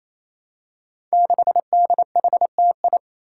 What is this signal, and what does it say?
Morse code, heard as a keyed tone, about 28 wpm, 700 Hz. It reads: 6DHTI
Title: Five Challenging Letter-Number Combinations 28wpm